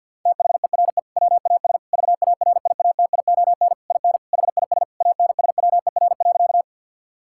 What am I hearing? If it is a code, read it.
Morse code, heard as a keyed tone, about 50 wpm, 700 Hz. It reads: THERE WAS VARIATION IN HIS ANSWER.
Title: Sentences from Top 1600 Words 50wpm